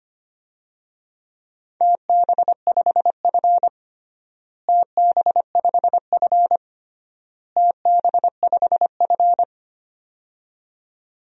Morse code, heard as a keyed tone, about 25 wpm, 700 Hz. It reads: TB5F TB5F TB5F
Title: DXCC-100 Call Signs - 4 Characters - Repeated 3x - 25wpm